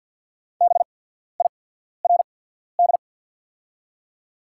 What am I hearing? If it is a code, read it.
Morse code, heard as a keyed tone, about 48 wpm, spaced as for 14 wpm, 690 Hz. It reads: BIRD
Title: 8x Character Spacing - Top 500 Words - 50wpm